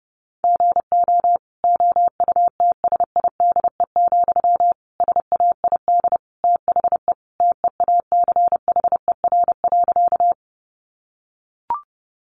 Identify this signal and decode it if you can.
Morse code, heard as a keyed tone, about 28 wpm, 700 Hz. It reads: GO OUTSIDE, SAID THE TEACHER.
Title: Sentences from Top 900 Words 30wpm